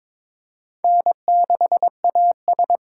Morse code, heard as a keyed tone, about 22 wpm, 700 Hz. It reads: N6AS
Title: USA Call Signs - 4 Characters - 3x Character Spacing - 22wpm